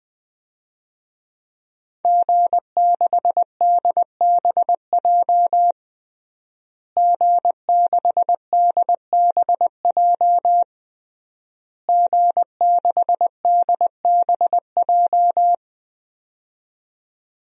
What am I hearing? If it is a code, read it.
Morse code, heard as a keyed tone, about 20 wpm, 700 Hz. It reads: G6DBJ G6DBJ G6DBJ
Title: DXCC-100 Call Signs - 5 Characters - Repeated 3x - 20wpm